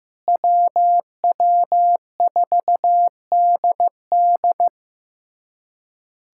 Morse code, heard as a keyed tone, about 16 wpm, 700 Hz. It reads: WW4DD
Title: USA Callsign DE Callsign - 5 Characters - 15wpm